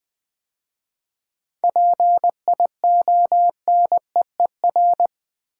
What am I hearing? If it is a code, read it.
Morse code, heard as a keyed tone, about 20 wpm, 700 Hz. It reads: PIONEER
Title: 7 Letter Words - First 3 Letters Given - 20wpm